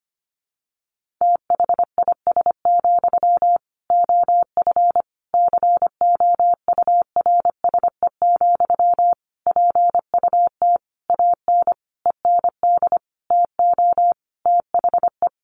Morse code, heard as a keyed tone, about 25 wpm, 700 Hz. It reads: THIS, OF COURSE, PUT AN END TO THE
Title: Sentences from Top 1400 Words 25wpm